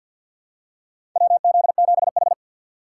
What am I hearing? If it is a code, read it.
Morse code, heard as a keyed tone, about 51 wpm, 700 Hz. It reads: WZ7H
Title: USA Call Signs - 4 Characters - 2x Character Spacing - 50wpm